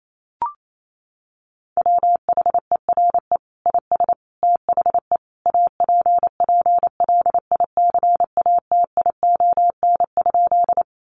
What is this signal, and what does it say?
Morse code, heard as a keyed tone, about 28 wpm, 700 Hz. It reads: WHERE IS THE APPLICATION?
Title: Sentences from Top 1200 Words 28wpm